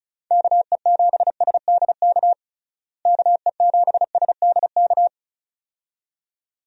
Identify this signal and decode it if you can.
Morse code, heard as a keyed tone, about 34 wpm, 700 Hz. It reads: KE7SDK KE7SDK
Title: USA Call Signs - 6 Characters - Repeated 2x - 35wpm